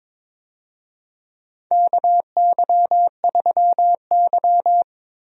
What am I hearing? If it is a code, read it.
Morse code, heard as a keyed tone, about 21 wpm, 700 Hz. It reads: KY3Y
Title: USA Call Signs - 4 Characters - Repeated 2x - 22wpm